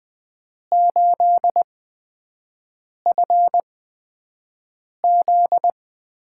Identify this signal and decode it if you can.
Morse code, heard as a keyed tone, about 20 wpm, spaced as for 5 wpm, 700 Hz. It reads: 8FZ
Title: USA Call Signs - 6 Characters - 8x Character Spacing - 20wpm